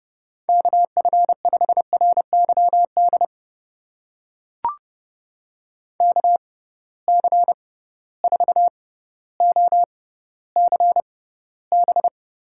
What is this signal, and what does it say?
Morse code, heard as a keyed tone, about 30 wpm, spaced as for 15 wpm, 700 Hz. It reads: KF5RYD KC4OCB
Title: USA Call Signs - 6 Characters - 6x Character Spacing - 30wpm